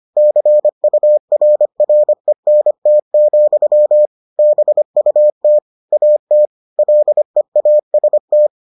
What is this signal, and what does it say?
Morse code, heard as a keyed tone, about 25 wpm, 600 Hz. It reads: CURRENT, BUT AT LEAST